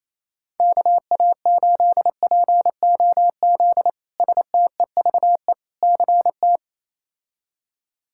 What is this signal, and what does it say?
Morse code, heard as a keyed tone, about 28 wpm, 700 Hz. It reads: KA8POZ STEVE CT
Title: K1USN SST QSO Practice - 28wpm